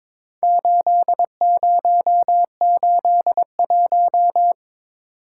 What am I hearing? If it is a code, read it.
Morse code, heard as a keyed tone, about 22 wpm, 700 Hz. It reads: 8081
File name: practice / 22wpm / Four Letter-Number Combinations - 22wpm